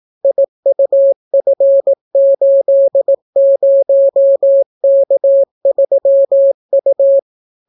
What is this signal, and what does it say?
Morse code, heard as a keyed tone, about 18 wpm, 550 Hz. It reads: IUF80K3U